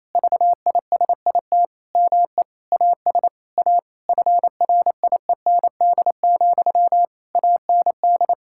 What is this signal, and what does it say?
Morse code, heard as a keyed tone, about 28 wpm, 700 Hz. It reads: VISIT ME AS A FRIEND, AND
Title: Sentences from Top 1000 Words - Review Part 6 - 28wpm